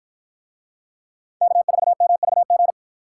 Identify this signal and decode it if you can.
Morse code, heard as a keyed tone, about 49 wpm, 700 Hz. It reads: K4MVG